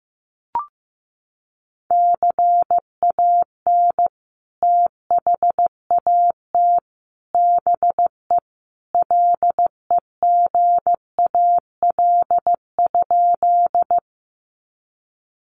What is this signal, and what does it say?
Morse code, heard as a keyed tone, about 15 wpm, 700 Hz. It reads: CAN THAT BE LEGAL?
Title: Sentences from Top 1200 Words 15wpm